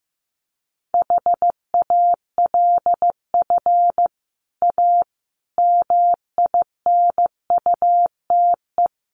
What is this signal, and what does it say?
Morse code, heard as a keyed tone, about 15 wpm, 700 Hz. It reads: HALF A MINUTE
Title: Sentences from Top 1100 Words 15wpm